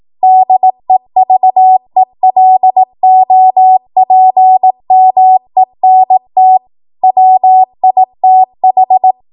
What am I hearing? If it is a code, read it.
Morse code, heard as a keyed tone, about 18 wpm, 750 Hz. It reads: DEVELOPMENT WITH
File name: Archive / 18 WPM / 18WPM